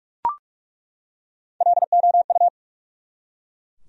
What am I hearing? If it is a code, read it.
Morse code, heard as a keyed tone, about 46 wpm, 700 Hz. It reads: LOU
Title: USA Common Names - 45wpm